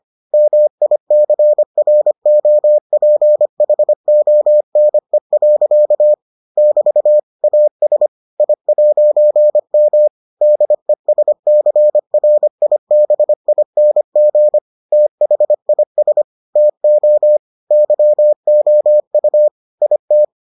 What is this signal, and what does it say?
Morse code, heard as a keyed tone, about 25 wpm, 600 Hz. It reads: MICROPHONE. <BT> AS I'M DESCRIBING THIS TO YOU IT